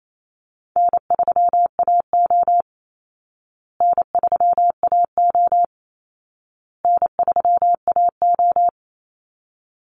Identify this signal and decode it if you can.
Morse code, heard as a keyed tone, about 28 wpm, 700 Hz. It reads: N3AO N3AO N3AO
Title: Easy Call Signs Repeated 3x - 28wpm